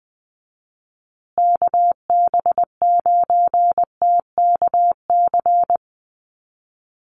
Morse code, heard as a keyed tone, about 20 wpm, 700 Hz. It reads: KB9TKC
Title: USA Call Signs Repeated 2x 20wpm